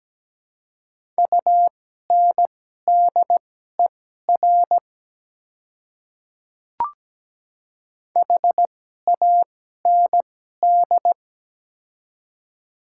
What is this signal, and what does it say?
Morse code, heard as a keyed tone, about 17 wpm, spaced as for 12 wpm, 700 Hz. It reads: UNDER HAND
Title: 2x Character Spacing - Top 200 Words - 17wpm